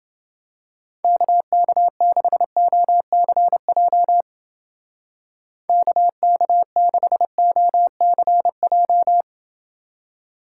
Morse code, heard as a keyed tone, about 28 wpm, 700 Hz. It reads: KK6OCJ KK6OCJ